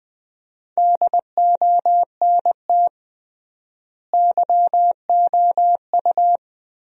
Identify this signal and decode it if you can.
Morse code, heard as a keyed tone, about 20 wpm, 700 Hz. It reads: DONT YOU
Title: 3x Word-Spacing - Sentences from Top 300 Words - 20wpm